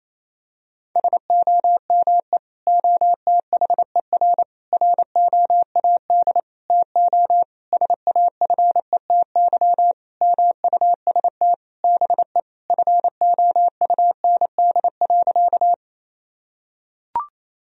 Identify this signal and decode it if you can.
Morse code, heard as a keyed tone, about 28 wpm, 700 Hz. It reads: SOME OTHER ROAD TO SAFETY MUST BE FOUND.
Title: Sentences from Top 1000 Words - Review Part 4 - 28wpm